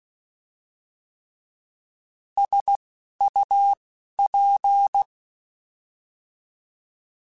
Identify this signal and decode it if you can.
Morse code, heard as a keyed tone, about 16 wpm, spaced as for 11 wpm, 800 Hz. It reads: SUP
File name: Alphabetical / Alphabetical mp3s / Ss @10